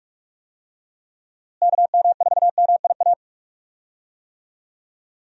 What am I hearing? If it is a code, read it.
Morse code, heard as a keyed tone, about 45 wpm, 700 Hz. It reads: KM4MIA
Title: USA Call Signs - 6 Characters - Repeated 3x - 45wpm